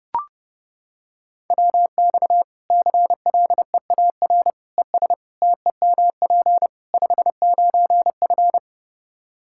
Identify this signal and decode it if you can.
Morse code, heard as a keyed tone, about 30 wpm, 700 Hz. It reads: WX CLEAR ES TEMP 59F
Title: Practice QSO - Exchange 2 - Info - 30wpm